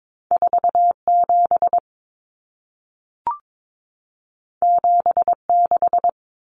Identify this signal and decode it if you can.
Morse code, heard as a keyed tone, about 22 wpm, 700 Hz. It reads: 47 76
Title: Numbers - 2-Digit 22wpm